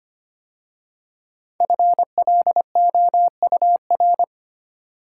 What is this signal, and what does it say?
Morse code, heard as a keyed tone, about 25 wpm, 700 Hz. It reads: FLOUR
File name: 5 Letter Words - First 3 Letters Given - 25wpm